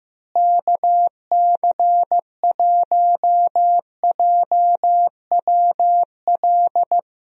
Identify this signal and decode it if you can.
Morse code, heard as a keyed tone, about 15 wpm, 700 Hz. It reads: KC1JWL
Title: USA Call Signs - 6 Characters - Repeated 2x - 15wpm